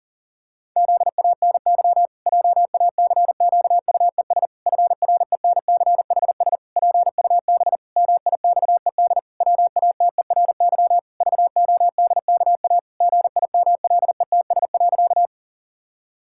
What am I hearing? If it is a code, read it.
Morse code, heard as a keyed tone, about 40 wpm, 700 Hz. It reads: ZANY JACQUES FRENCHS PUB MIXED WATERY VODKA GIMLETS.
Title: Pangram Challenge v2 - 40wpm